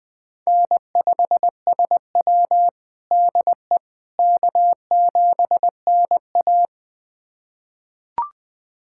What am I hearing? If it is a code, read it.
Morse code, heard as a keyed tone, about 21 wpm, 700 Hz. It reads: N5SW DE K7NA
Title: USA Callsign DE Callsign - 4 Characters - 20wpm